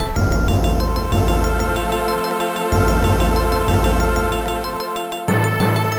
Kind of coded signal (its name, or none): none